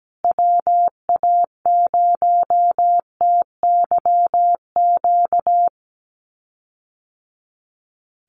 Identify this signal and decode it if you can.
Morse code, heard as a keyed tone, about 17 wpm, 700 Hz. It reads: WA0TYQ
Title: USA Call Signs Repeated 2x 17wpm